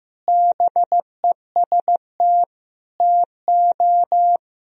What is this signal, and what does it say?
Morse code, heard as a keyed tone, about 15 wpm, 700 Hz. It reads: BEST TO
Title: Sentences from Top 1000 Words - Review Part 2 - 15wpm